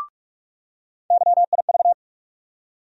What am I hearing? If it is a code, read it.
Morse code, heard as a keyed tone, about 46 wpm, 700 Hz. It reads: YIV
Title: USA Call Sign Postfixes - 3 Letters 45wpm